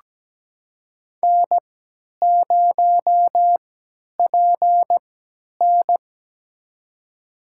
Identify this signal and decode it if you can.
Morse code, heard as a keyed tone, about 17 wpm, spaced as for 10 wpm, 700 Hz. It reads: N0PN
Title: USA Call Signs - 3x Character Spacing - 17wpm